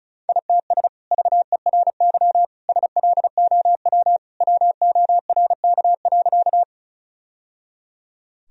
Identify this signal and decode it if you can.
Morse code, heard as a keyed tone, about 36 wpm, 700 Hz. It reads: ITS VERY SLOW WORK.
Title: Sentences from Top 1000 Words - Review Part 2 - 35wpm